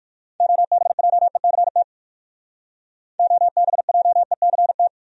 Morse code, heard as a keyed tone, about 58 wpm, 700 Hz. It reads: OBJECT OBJECT